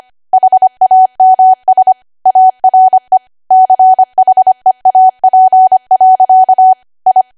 Morse code, heard as a keyed tone, about 25 wpm, 750 Hz. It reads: HAMS ARE CHEAP. I
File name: Archive / 25 WPM / 25WPM